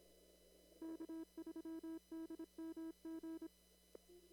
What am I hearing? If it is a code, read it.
Morse code, heard as a keyed tone, about 26 wpm, 320 Hz. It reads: K3DMG
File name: CW / kt5j wpx cw-op-k5pi